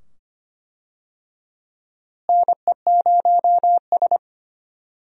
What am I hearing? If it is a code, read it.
Morse code, heard as a keyed tone, about 25 wpm, 700 Hz. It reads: NE0S
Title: USA Call Signs - 4 Characters - Repeated 2x - 25wpm